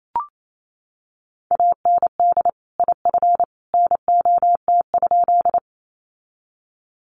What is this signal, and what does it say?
Morse code, heard as a keyed tone, about 28 wpm, 700 Hz. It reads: AND IF NOT?
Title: Sentences from Top 600 Words 28wpm